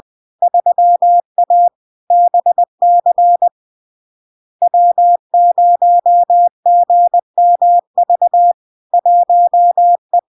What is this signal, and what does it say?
Morse code, heard as a keyed tone, about 20 wpm, 700 Hz. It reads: 3A BC W0GMV 1E
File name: LARK-FD-2022-Log-in-CW-20wpm-700-Hz